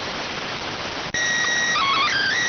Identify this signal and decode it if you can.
SSTV (slow-scan television), PD 290.